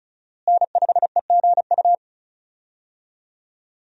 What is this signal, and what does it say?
Morse code, heard as a keyed tone, about 36 wpm, 700 Hz. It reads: N5EGU
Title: USA Call Signs - 5 Characters - Repeated 3x - 35wpm